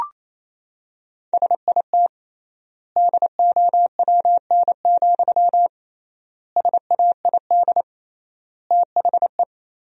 Morse code, heard as a keyed tone, about 28 wpm, 700 Hz. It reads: SIT DOWN, SAID THE
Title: 3x Word-Spacing - Sentences from Top 500 Words - 28wpm